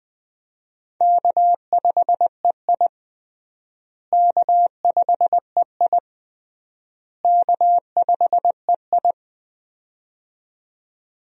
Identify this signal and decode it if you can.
Morse code, heard as a keyed tone, about 20 wpm, 700 Hz. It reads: K5EI K5EI K5EI
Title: USA Call Signs - 4 Characters - Repeated 3x - 20wpm